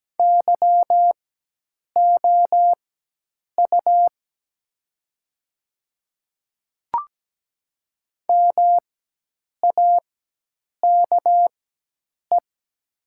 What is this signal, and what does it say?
Morse code, heard as a keyed tone, about 17 wpm, spaced as for 8 wpm, 700 Hz. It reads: YOU MAKE
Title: 4x Character Spacing - Top 100 Words - 17wpm